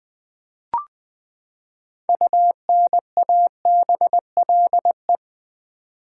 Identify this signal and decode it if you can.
Morse code, heard as a keyed tone, about 20 wpm, 700 Hz. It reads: UNABLE
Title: Six Letter Words - 20wpm